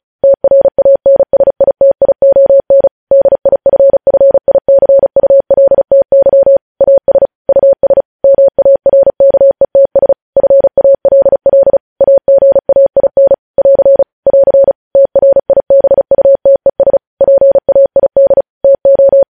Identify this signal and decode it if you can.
Morse code, heard as a keyed tone, about 36 wpm, 550 Hz. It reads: TRANSITION DIFFICULTY AS US MARKETS FALL AGAIN <AR> <AR> TRIBUTES PAID TO
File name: podcast / News headlines-1x-35x35